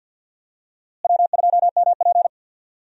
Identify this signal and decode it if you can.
Morse code, heard as a keyed tone, about 53 wpm, 700 Hz. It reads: W2MP
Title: USA Call Signs - 4 Characters - Repeated 3x - 50wpm